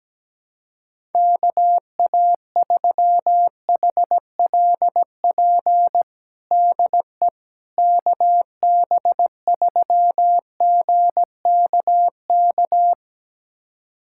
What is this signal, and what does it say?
Morse code, heard as a keyed tone, about 17 wpm, 700 Hz. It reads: KA3HLP DE KB3GKK
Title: USA Callsign DE Callsign 17wpm